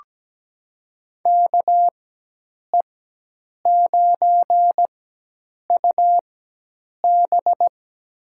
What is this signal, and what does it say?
Morse code, heard as a keyed tone, about 17 wpm, spaced as for 8 wpm, 700 Hz. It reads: KE9UB
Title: USA Call Signs - 5 Characters - 4x Character Spacing - 17wpm